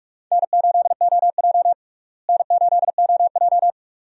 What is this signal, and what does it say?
Morse code, heard as a keyed tone, about 46 wpm, 700 Hz. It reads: N8OJ N8OJ
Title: USA Call Signs - 4 Characters - Repeated 2x - 45wpm